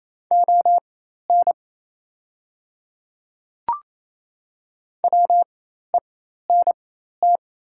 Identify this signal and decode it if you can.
Morse code, heard as a keyed tone, about 27 wpm, spaced as for 13 wpm, 700 Hz. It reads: ON WENT
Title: 4x Character Spacing - Top 200 Words - 28wpm